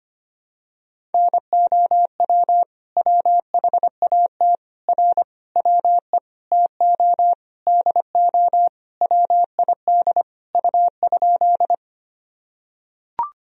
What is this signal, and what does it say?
Morse code, heard as a keyed tone, about 25 wpm, 700 Hz. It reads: NOW WHAT R WE TO DO WID U?
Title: Sentences from Top 100 Words and Abbreviations 25wpm